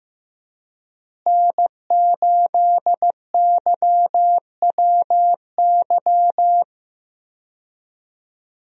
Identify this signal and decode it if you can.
Morse code, heard as a keyed tone, about 15 wpm, 700 Hz. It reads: N8YWY
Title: USA Call Signs - 5 Characters - Repeated 3x - 15wpm